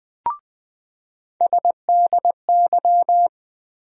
Morse code, heard as a keyed tone, about 20 wpm, 700 Hz. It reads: SDY
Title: USA Call Sign Postfixes - 3 Letters 20wpm